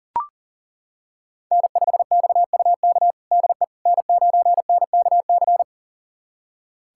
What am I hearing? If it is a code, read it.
Morse code, heard as a keyed tone, about 40 wpm, 700 Hz. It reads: N5XUK DE N9NKC
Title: USA Callsign DE Callsign - 5 Characters - 40wpm